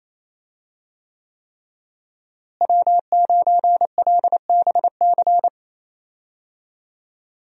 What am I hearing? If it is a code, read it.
Morse code, heard as a keyed tone, about 28 wpm, 700 Hz. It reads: W9LBC
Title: USA Call Signs - 5 Characters - Repeated 3x - 28wpm